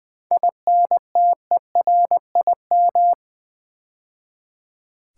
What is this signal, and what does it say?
Morse code, heard as a keyed tone, about 20 wpm, 700 Hz. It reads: INTERIM
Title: 7 Letter Words - First 3 Letters Given - 20wpm